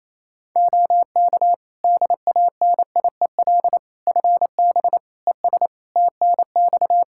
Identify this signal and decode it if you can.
Morse code, heard as a keyed tone, about 28 wpm, 700 Hz. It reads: OK DANIEL FB ES TNX
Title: Practice QSO - Exchange 2 - Info - 28wpm